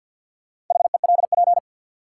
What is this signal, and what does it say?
Morse code, heard as a keyed tone, about 50 wpm, 700 Hz. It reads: HELP
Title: Sets of 2 Words - New Word Drill - 50wpm